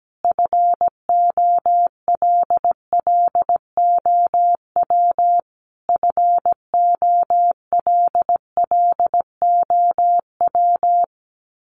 Morse code, heard as a keyed tone, about 17 wpm, 700 Hz. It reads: FOLLOW FOLLOW
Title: Top 200 Words Repeated 17wpm